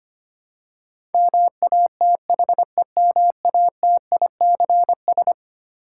Morse code, heard as a keyed tone, about 25 wpm, 700 Hz. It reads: MATHEMATICS
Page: morse-code-ninja-practice-25wpm